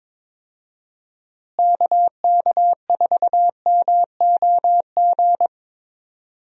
Morse code, heard as a keyed tone, about 22 wpm, 700 Hz. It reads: KK4MOG
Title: USA Call Signs - 6 Characters - Repeated 3x - 22wpm